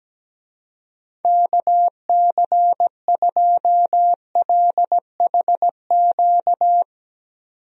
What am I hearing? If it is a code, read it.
Morse code, heard as a keyed tone, about 17 wpm, 700 Hz. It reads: KC2LHQ
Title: USA Call Signs - 6 Characters - 17wpm